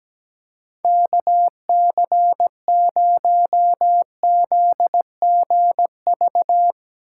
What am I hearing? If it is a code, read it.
Morse code, heard as a keyed tone, about 17 wpm, 700 Hz. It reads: KC0ZGV
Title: Speed Racing - USA 2x3 Call Signs 17wpm